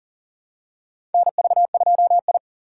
Morse code, heard as a keyed tone, about 40 wpm, 700 Hz. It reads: NV2I
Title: USA Call Signs - 4 Characters - 2x Character Spacing - 40wpm